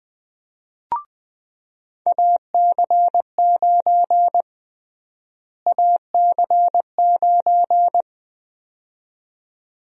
Morse code, heard as a keyed tone, about 20 wpm, 700 Hz. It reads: AC9 AC9
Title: USA Call Sign Prefixes - Repeated 2x - 20wpm